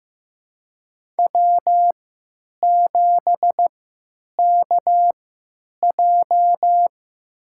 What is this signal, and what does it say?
Morse code, heard as a keyed tone, about 15 wpm, spaced as for 9 wpm, 700 Hz. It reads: W7KJ
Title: USA Call Signs - 4 Characters - 3x Character Spacing - 15wpm